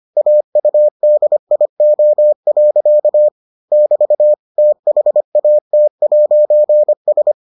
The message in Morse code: AUDIO. <BT> THAT'S

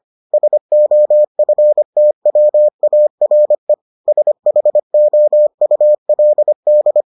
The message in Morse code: SOFTWARE SHOULD

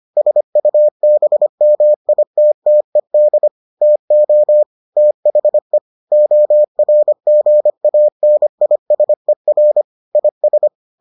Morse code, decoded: SUBMITTED TO THE ORGANISER IS

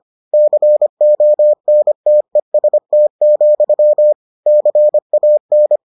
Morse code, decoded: CONTEST, CAN

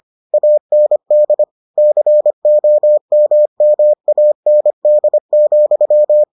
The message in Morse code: AND COMMAND,

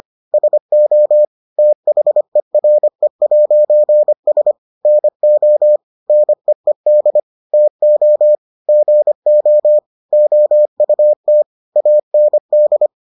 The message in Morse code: SO THERE'S NO NEED TO GO OUT AND

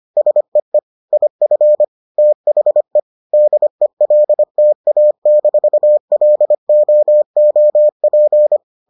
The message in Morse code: SEE IF THE DELTA-LOOP

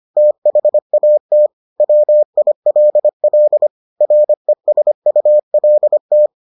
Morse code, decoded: THAT WILL RESULT

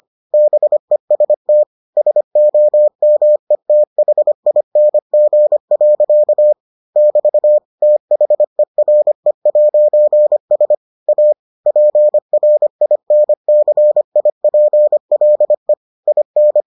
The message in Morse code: BEST SOMETHING. <BT> THERE'S A PRINCIPLE IN

